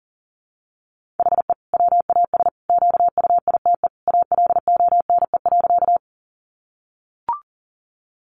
Morse code, heard as a keyed tone, about 40 wpm, 700 Hz. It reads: HE WAS QUITE ALONE.
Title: Sentences from Top 1000 Words 40wpm